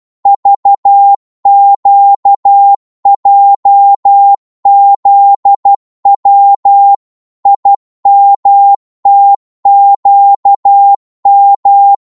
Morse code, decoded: VQJZW IMTQM